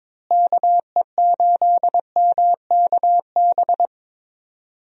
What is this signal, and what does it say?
Morse code, heard as a keyed tone, about 22 wpm, 700 Hz. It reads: KE8MKB
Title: USA Call Signs - 6 Characters - Repeated 2x - 22wpm